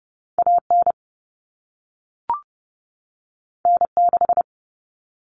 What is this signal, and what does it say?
Morse code, heard as a keyed tone, about 30 wpm, 700 Hz. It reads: AN N6
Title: Common Cut Numbers - 2-Digit 30wpm